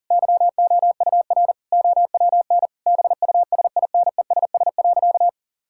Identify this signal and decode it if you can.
Morse code, heard as a keyed tone, about 40 wpm, 700 Hz. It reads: YOUR OWN BUSINESS.
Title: Sentences from Top 1000 Words - Review Part 5 - 40wpm